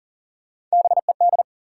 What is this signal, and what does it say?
Morse code, heard as a keyed tone, about 40 wpm, 700 Hz. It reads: BED
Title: Sets of 5 Words Encore - New Word Drill - 40wpm